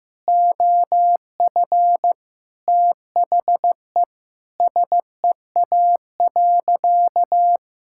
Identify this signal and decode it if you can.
Morse code, heard as a keyed tone, about 16 wpm, 700 Hz. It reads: OF THE SEA.
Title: Sentences from Top 1000 Words - Review Part 5 - 15wpm